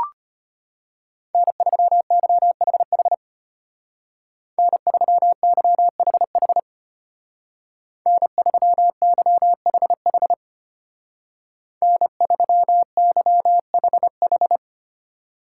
Morse code, heard as a keyed tone, about 32 wpm, 700 Hz. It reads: N3YHH N3YHH N3YHH N3YHH